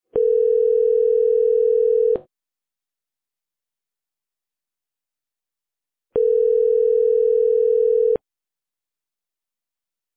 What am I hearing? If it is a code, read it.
Ringback tone (North America), 440 + 480 Hz.